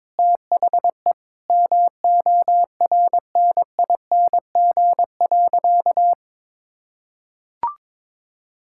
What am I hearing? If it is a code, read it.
Morse code, heard as a keyed tone, about 22 wpm, 700 Hz. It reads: THE MORNING.
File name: practice / 22wpm / Sentences with 3-Word Phrases - 22wpm